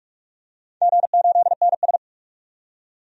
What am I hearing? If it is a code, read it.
Morse code, heard as a keyed tone, about 46 wpm, 700 Hz. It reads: G8NS
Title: DXCC-100 Call Signs - 4 Characters - 45wpm